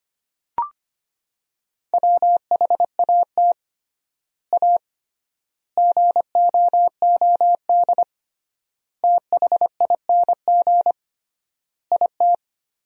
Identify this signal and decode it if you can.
Morse code, heard as a keyed tone, about 25 wpm, 700 Hz. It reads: WHAT A GOOD THING IT
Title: 3x Word-Spacing - Sentences from Top 200 Words - 25wpm